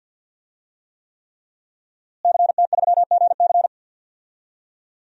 Morse code, heard as a keyed tone, about 50 wpm, 700 Hz. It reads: CT3GC
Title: DXCC-100 Call Signs - 5 Characters - Repeated 3x - 50wpm